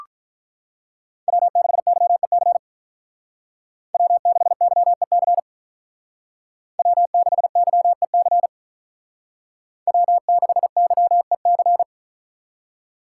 Morse code, heard as a keyed tone, about 45 wpm, 700 Hz. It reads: W6YEC W6YEC W6YEC W6YEC